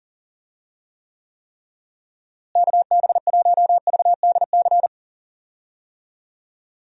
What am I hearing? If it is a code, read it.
Morse code, heard as a keyed tone, about 39 wpm, 700 Hz. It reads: KB1VDC
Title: USA Call Signs - 6 Characters - Repeated 3x - 40wpm